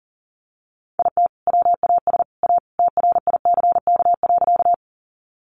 Morse code, heard as a keyed tone, about 40 wpm, 700 Hz. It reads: IT WAS A TRICK.
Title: Sentences from Top 1500 Words 40wpm